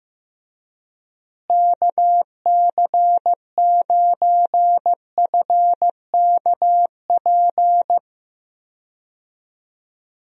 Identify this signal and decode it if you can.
Morse code, heard as a keyed tone, about 15 wpm, 700 Hz. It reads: KC9FKP
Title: USA Call Signs - 6 Characters - Repeated 2x - 15wpm